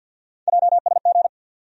Morse code, heard as a keyed tone, about 50 wpm, 700 Hz. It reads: JSG
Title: USA Call Sign Postfixes - 3 Letters 50wpm